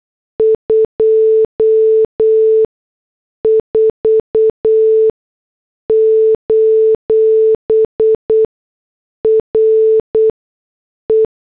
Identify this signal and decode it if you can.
Morse code, heard as a keyed tone, about 8 wpm, spaced as for 6 wpm, 440 Hz. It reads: 24: RE
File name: morse / headlines